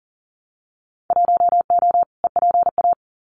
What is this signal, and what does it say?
Morse code, heard as a keyed tone, about 40 wpm, 700 Hz. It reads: 1O EPA
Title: Winter Field Day Exchanges 40wpm